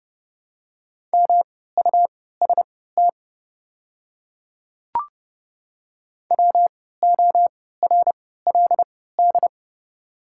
Morse code, heard as a keyed tone, about 30 wpm, spaced as for 17 wpm, 700 Hz. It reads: MUST WORLD
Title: 3x Character Spacing - Top 500 Words - 30wpm